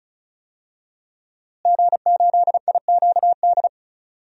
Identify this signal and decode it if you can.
Morse code, heard as a keyed tone, about 36 wpm, 700 Hz. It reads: G8IQD
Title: DXCC-100 Call Signs - 5 Characters - Repeated 2x - 35wpm